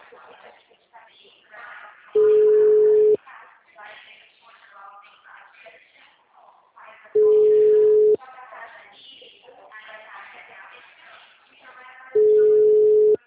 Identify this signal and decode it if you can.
Ringback tone (Europe, 425 Hz), with speech.
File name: out- -2041- - - .64283